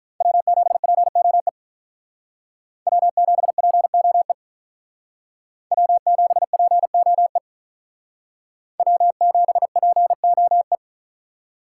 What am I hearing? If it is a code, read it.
Morse code, heard as a keyed tone, about 45 wpm, 700 Hz. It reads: W7POE W7POE W7POE W7POE